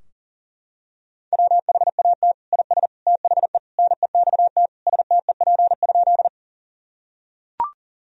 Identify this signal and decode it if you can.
Morse code, heard as a keyed tone, about 37 wpm, 700 Hz. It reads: WHAT IS THE NEXT STEP?
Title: Sentences from Top 1000 Words - Review Part 2 - 40wpm